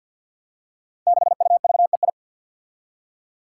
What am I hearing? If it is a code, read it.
Morse code, heard as a keyed tone, about 47 wpm, 700 Hz. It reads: 6UVEI
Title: Five Challenging Letter-Number Combinations 50wpm